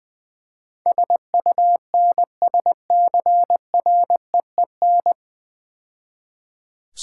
Morse code, heard as a keyed tone, about 22 wpm, 700 Hz. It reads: SUNSCREEN
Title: Easy Words - Speed-Racing with Hints - 20wpm